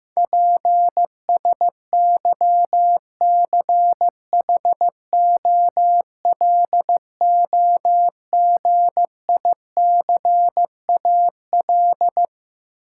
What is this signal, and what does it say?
Morse code, heard as a keyed tone, about 15 wpm, 700 Hz. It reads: PSYCHOLOGICAL
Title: Sentences from Top 1400 Words - New Word Drill Alt - 15wpm